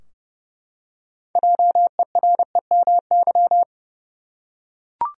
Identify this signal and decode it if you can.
Morse code, heard as a keyed tone, about 29 wpm, 700 Hz. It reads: JEREMY
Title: USA Common Names - 30wpm